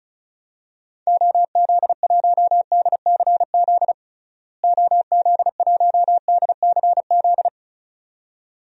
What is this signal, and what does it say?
Morse code, heard as a keyed tone, about 36 wpm, 700 Hz. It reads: OZ1DCZ OZ1DCZ
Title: DXCC-100 Call Signs - 6 Characters - Repeated 2x - 35wpm